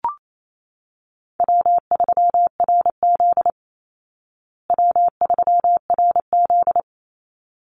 Morse code, heard as a keyed tone, about 28 wpm, 700 Hz. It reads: W3RZ W3RZ
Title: Easy Call Signs Repeated 2x - 28wpm